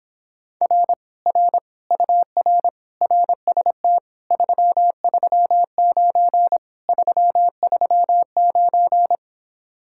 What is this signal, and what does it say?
Morse code, heard as a keyed tone, about 26 wpm, 700 Hz. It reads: R R UR RST 339 339